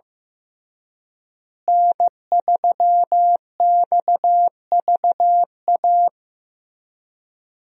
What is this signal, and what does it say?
Morse code, heard as a keyed tone, about 15 wpm, 700 Hz. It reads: N3XVA
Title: USA Call Signs - 5 Characters - Repeated 3x - 15wpm